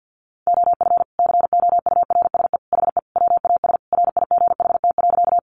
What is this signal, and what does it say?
Morse code, heard as a keyed tone, about 50 wpm, 700 Hz. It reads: OF COURSE HE WAS RIGHT.
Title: Sentences from Top 500 Words 50wpm